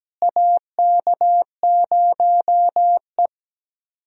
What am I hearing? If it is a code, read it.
Morse code, heard as a keyed tone, about 17 wpm, 700 Hz. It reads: AK0E
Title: USA Call Signs - 4 Characters - Repeated 3x - 17wpm